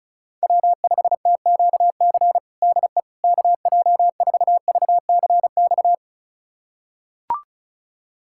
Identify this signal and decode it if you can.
Morse code, heard as a keyed tone, about 34 wpm, 690 Hz. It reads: W5TQC DE KJ4VCX
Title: USA Callsign DE Callsign 35wpm